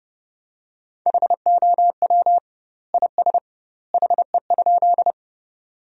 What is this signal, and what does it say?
Morse code, heard as a keyed tone, about 31 wpm, 700 Hz. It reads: HOW IS HE?
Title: 2x Word-Spacing - Sentences from Top 100 Words - 30wpm